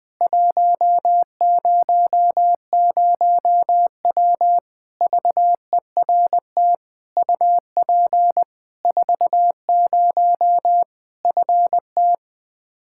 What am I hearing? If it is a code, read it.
Morse code, heard as a keyed tone, about 20 wpm, 700 Hz. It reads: 100W VERT UP 40 FT
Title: K1USN SST QSO Practice 20wpm